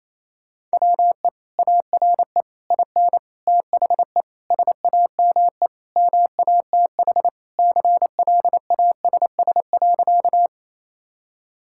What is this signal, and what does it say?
Morse code, heard as a keyed tone, about 27 wpm, 700 Hz. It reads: WE ARE IN THE SAME MATH CLASS.
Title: Sentences from Top 1000 Words - Review Part 6 - 28wpm